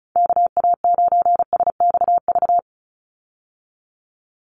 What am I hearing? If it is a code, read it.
Morse code, heard as a keyed tone, about 34 wpm, 700 Hz. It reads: KA9SXV
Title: Call Signs - 35wpm